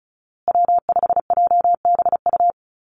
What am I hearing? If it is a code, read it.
Morse code, heard as a keyed tone, about 34 wpm, 700 Hz. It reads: W5JBU
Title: USA Call Signs Repeated 3x 35wpm